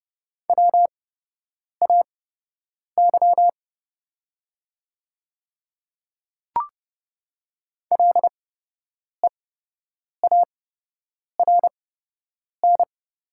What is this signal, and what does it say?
Morse code, heard as a keyed tone, about 30 wpm, spaced as for 8 wpm, 700 Hz. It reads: WAY LEARN
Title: 8x Character Spacing - Top 300 Words - 30wpm